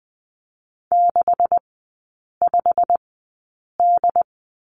Morse code, heard as a keyed tone, about 20 wpm, spaced as for 8 wpm, 690 Hz. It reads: 65D
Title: 159 Lesson 26e - Letter g - Three Characters 20wpm